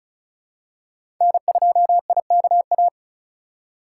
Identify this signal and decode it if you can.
Morse code, heard as a keyed tone, about 36 wpm, 690 Hz. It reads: N2IKA
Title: USA Call Signs - 3x Character Spacing - 35wpm